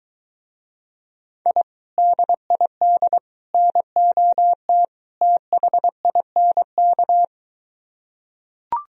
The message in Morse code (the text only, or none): I DID NOT THINK